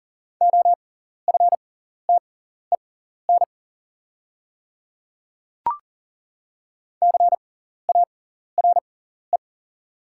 Morse code, heard as a keyed tone, about 40 wpm, spaced as for 14 wpm, 700 Hz. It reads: OFTEN CARE
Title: 6x Character Spacing - Top 300 Words - 40wpm